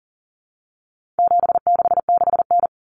Morse code, heard as a keyed tone, about 40 wpm, 700 Hz. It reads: 766N
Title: Common Cut Numbers - 4-Digit 40wpm